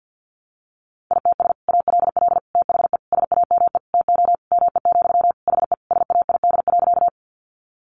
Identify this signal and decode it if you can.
Morse code, heard as a keyed tone, about 50 wpm, 700 Hz. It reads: ITS ALL THE SAME TO ME, HE SAID.
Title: Sentences from Top 600 Words 50wpm